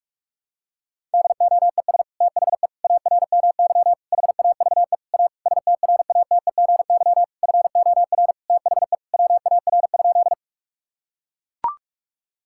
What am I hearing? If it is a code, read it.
Morse code, heard as a keyed tone, about 42 wpm, 700 Hz. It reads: DOES THE ARMY HAVE A STRATEGY FOR THE WAR?
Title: Sentences from Top 1000 Words - Review Part 5 - 45wpm